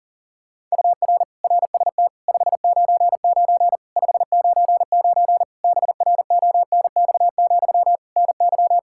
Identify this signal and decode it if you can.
Morse code, heard as a keyed tone, about 40 wpm, 700 Hz. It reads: UR RST 599 599 BRONX, NY
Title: Practice QSO - Exchange 1 - Beginning - 40wpm